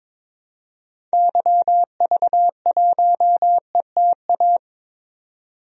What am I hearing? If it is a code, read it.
Morse code, heard as a keyed tone, about 22 wpm, 700 Hz. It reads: YV1ETA
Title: DXCC-100 Call Signs - 6 Characters - 22wpm